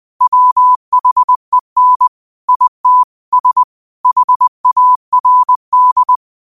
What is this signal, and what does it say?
Morse code, heard as a keyed tone, about 20 wpm, 1000 Hz. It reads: WHEN IT S HARD
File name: qotdfiles / 20wpm